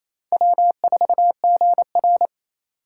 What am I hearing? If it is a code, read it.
Morse code, heard as a keyed tone, about 32 wpm, 700 Hz. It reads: W4GR
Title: Speed Racing - USA 1x2 Call Signs 28wpm